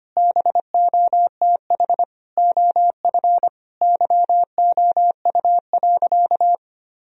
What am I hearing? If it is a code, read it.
Morse code, heard as a keyed tone, about 25 wpm, 700 Hz. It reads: BOTH OF YOU.
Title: Sentences from Top 1000 Words - Review Part 1 - 25wpm